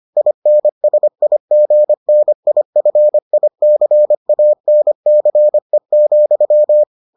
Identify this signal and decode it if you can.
Morse code, heard as a keyed tone, about 25 wpm, 600 Hz. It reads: INSIGNIFICANCE,